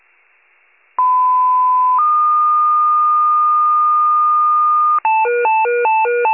Two-tone page, heard as a one-second tone, then a three-second tone, about 1010 then 1220 Hz. It is heard from 1.0 to 5.0 s.